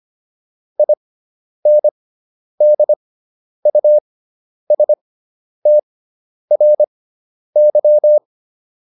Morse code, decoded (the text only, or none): INDUSTRY